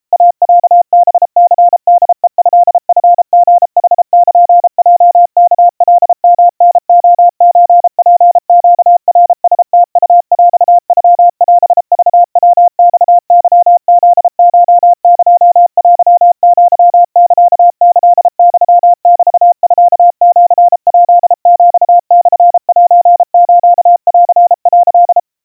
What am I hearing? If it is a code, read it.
Morse code (Japanese Wabun code), heard as a keyed tone, about 34 wpm, 700 Hz. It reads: イロハニホヘトチリヌルヲワカヨタレソツネナラムウヰノオクヤマケフコエテアサキユメミシヱヒモセスン」
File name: iroha 110 cpm 700 Hz Crispy